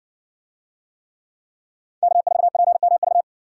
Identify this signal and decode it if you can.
Morse code, heard as a keyed tone, about 63 wpm, 700 Hz. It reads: K4PMV